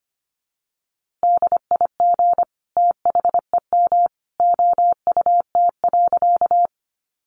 Morse code, heard as a keyed tone, about 25 wpm, 700 Hz. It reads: DIG THEM OUT.
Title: A Sound Of Thunder 25wpm